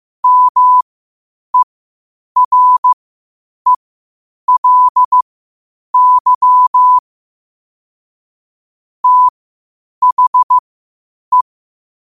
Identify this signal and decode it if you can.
Morse code, heard as a keyed tone, about 15 wpm, spaced as for 8 wpm, 1000 Hz. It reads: MERELY THE
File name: qotdfiles / 8wpm